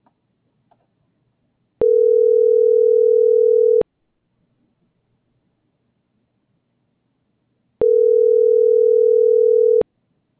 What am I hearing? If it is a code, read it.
Ringback tone (North America), 440 + 480 Hz.